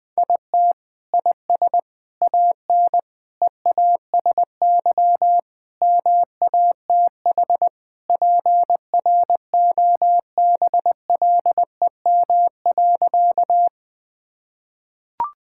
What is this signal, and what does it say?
Morse code, heard as a keyed tone, about 21 wpm, 700 Hz. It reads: IT IS AN EASY MATH PROBLEM.
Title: Sentences from Top 1000 Words - Review Part 6 - 20wpm